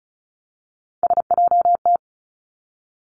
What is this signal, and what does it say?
Morse code, heard as a keyed tone, about 36 wpm, 700 Hz. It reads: SJT
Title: Three Challenging Letter-Number Combinations 35wpm